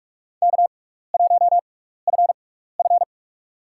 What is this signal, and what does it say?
Morse code, heard as a keyed tone, about 42 wpm, spaced as for 15 wpm, 700 Hz. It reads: K1FF
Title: USA Call Signs - 4 Characters - 6x Character Spacing - 45wpm